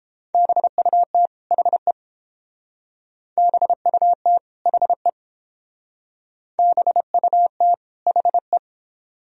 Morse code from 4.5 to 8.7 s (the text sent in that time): HE BUT HE